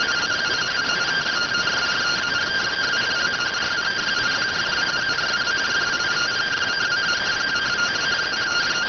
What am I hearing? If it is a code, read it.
Two-tone teleprinter signal, about 170 Hz shift.